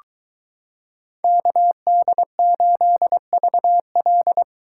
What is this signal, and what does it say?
Morse code, heard as a keyed tone, about 22 wpm, 700 Hz. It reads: KD8VL